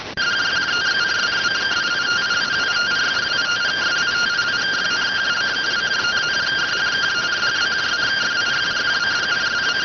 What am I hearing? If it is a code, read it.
Two-tone teleprinter signal, about 170 Hz shift.